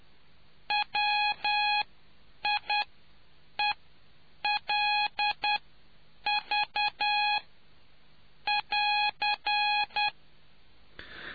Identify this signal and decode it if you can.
Morse code, heard as a keyed tone, about 10 wpm, spaced as for 7 wpm, 830 Hz. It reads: WIELV <AR>